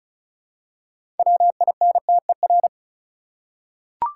Morse code, heard as a keyed tone, about 36 wpm, 700 Hz. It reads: WINTER
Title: Six Letter Words - 35wpm